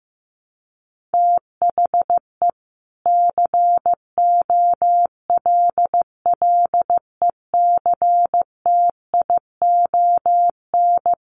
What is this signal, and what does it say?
Morse code, heard as a keyed tone, about 14 wpm, 700 Hz. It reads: THE COLLECTION
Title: Sentences from Top 1200 Words 15wpm